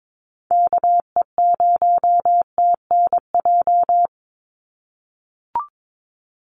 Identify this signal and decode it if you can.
Morse code, heard as a keyed tone, about 22 wpm, 700 Hz. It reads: KE0TNJ
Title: USA Call Signs 22wpm